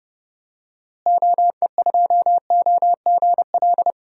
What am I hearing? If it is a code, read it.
Morse code, heard as a keyed tone, about 30 wpm, 700 Hz. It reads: OE2OGL
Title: DXCC-100 Call Signs - 6 Characters - Repeated 2x - 30wpm